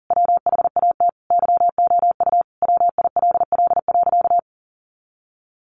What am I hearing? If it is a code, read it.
Morse code, heard as a keyed tone, about 37 wpm, 700 Hz. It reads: WHAT YOU WILL.
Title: Sentences from Top 200 Words 40wpm